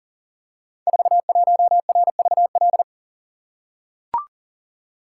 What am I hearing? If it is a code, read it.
Morse code, heard as a keyed tone, about 38 wpm, 700 Hz. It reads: 41RVL
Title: Code Groups - 5 Letter:Numbers - 40wpm